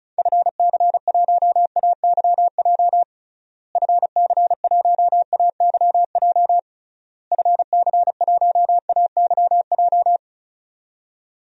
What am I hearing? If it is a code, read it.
Morse code, heard as a keyed tone, about 36 wpm, 700 Hz. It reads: FC1AYJ FC1AYJ FC1AYJ
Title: DXCC-100 Call Signs - 6 Characters - Repeated 3x - 35wpm